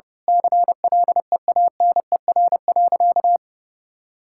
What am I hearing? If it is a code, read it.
Morse code, heard as a keyed tone, about 32 wpm, 700 Hz. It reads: CLEANER.
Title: Pangram Challenge v2 - 30wpm